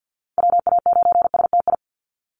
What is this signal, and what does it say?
Morse code, heard as a keyed tone, about 50 wpm, 700 Hz. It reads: WA9STI
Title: NA SOTA Activator Callsigns 50wpm